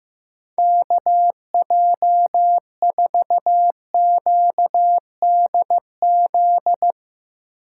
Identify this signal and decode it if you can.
Morse code, heard as a keyed tone, about 15 wpm, 700 Hz. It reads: KJ4QDZ
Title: USA Call Signs - 6 Characters - 2x Character Spacing - 15wpm